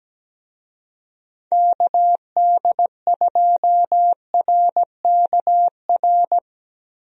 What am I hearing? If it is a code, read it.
Morse code, heard as a keyed tone, about 17 wpm, 700 Hz. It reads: KD2RKR
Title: USA Call Signs - 6 Characters - Repeated 2x - 17wpm